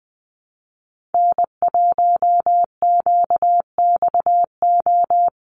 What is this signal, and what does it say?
Morse code, heard as a keyed tone, about 20 wpm, 700 Hz. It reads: N1QXO
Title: USA Call Signs Repeated 2x 20wpm